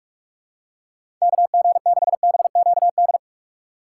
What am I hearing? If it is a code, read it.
Morse code, heard as a keyed tone, about 46 wpm, 700 Hz. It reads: KG6BQD
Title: USA Call Signs - 4x Character Spacing - 45wpm